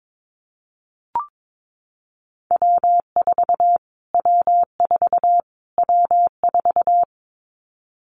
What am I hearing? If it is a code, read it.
Morse code, heard as a keyed tone, about 22 wpm, 700 Hz. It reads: W4 W4 W4
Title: Callsign Prefixes for Top 25 DXCC - Repeated 3x - 22wpm